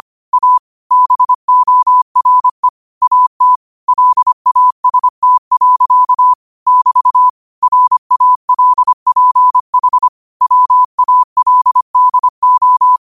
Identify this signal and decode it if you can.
Morse code, heard as a keyed tone, about 25 wpm, 1000 Hz. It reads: A BORE AT LAST. <BT> RALPH WALDO
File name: qotdfiles / 25wpm